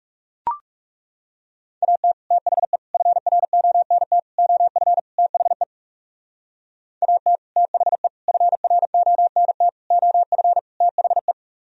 Morse code, heard as a keyed tone, about 38 wpm, 700 Hz. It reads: AT THE FRONT OF THE AT THE FRONT OF THE
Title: Speed-Racing - Sets of 5 Words 30wpm